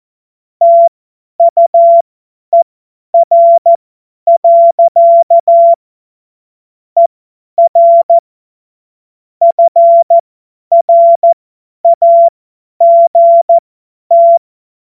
Morse code, heard as a keyed tone, about 14 wpm, spaced as for 10 wpm, 680 Hz. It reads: TUER. ER FRAGT
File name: Morsekurs / Witze / Witz 11. Januar 2022, 50 F70